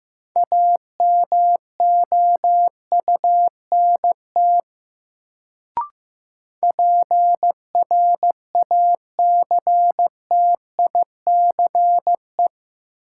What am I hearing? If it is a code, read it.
Morse code, heard as a keyed tone, about 15 wpm, 700 Hz. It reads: AMOUNT PRACTICE
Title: Sentences from Top 700 Words - New Word Drill - 15wpm